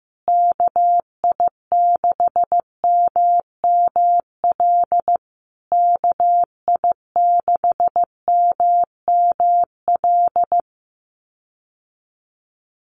Morse code, heard as a keyed tone, about 15 wpm, 700 Hz. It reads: KI6MML KI6MML
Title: USA Call Signs Repeated 3x 15wpm